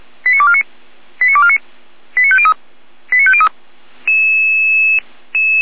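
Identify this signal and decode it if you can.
Five-tone selective call (ZVEI), heard as quick runs of five tones, tones of about 70 ms. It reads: 89249 89249 89503 89503
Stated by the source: ZVEI five-tone selective call 89276 received by SDR.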